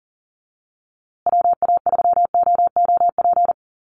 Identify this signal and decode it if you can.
Morse code, heard as a keyed tone, about 40 wpm, 700 Hz. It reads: WA3OOP
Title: USA Call Signs Repeated 3x 40wpm